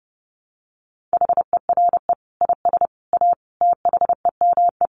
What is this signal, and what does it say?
Morse code, heard as a keyed tone, about 30 wpm, 700 Hz. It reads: HERE IS A THEME
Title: Sentences from Top 1100 Words 30wpm